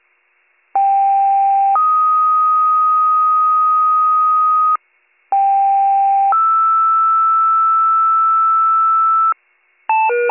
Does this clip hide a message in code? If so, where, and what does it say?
Yes: two-tone page from 0.7 to 9.3 s, heard as a one-second tone, then a three-second tone, about 790 then 1250 Hz, about 790 then 1320 Hz.